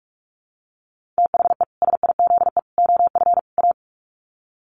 Morse code, heard as a keyed tone, about 44 wpm, 700 Hz. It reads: THE SIZE OF A
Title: Sets of 4 Words 45wpm